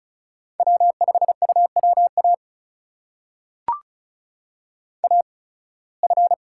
Morse code, heard as a keyed tone, about 34 wpm, spaced as for 23 wpm, 700 Hz. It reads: W5UWA AF
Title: USA Call Signs - 5 Characters - 8x Character Spacing - 35wpm